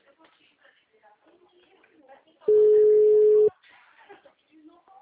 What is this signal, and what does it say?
Ringback tone (Europe, 425 Hz), with speech.